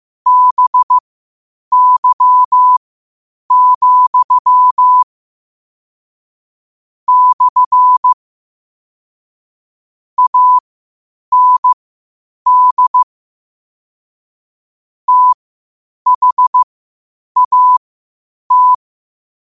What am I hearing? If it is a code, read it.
Morse code, heard as a keyed tone, about 15 wpm, spaced as for 8 wpm, 1000 Hz. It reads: BY, / AND THAT
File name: qotdfiles / 8wpm